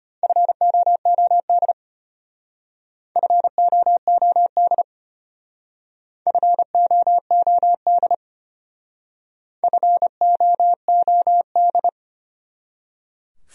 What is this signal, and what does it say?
Morse code, heard as a keyed tone, about 33 wpm, 700 Hz. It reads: FOOD FOOD FOOD FOOD